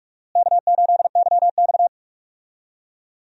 Morse code, heard as a keyed tone, about 41 wpm, 700 Hz. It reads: K8YX
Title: USA Call Signs - 4 Characters - Repeated 2x - 45wpm